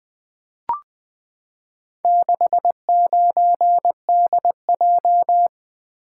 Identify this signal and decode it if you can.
Morse code, heard as a keyed tone, about 20 wpm, 700 Hz. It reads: 69DJ
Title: Four Letter-Number Combinations - 20wpm